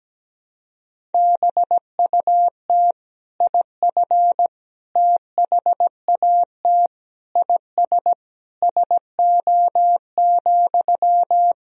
Morse code, heard as a keyed tone, about 17 wpm, 700 Hz. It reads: BUT IF THAT IS SO,
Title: Sentences from Top 200 Words and Abbreviations 17wpm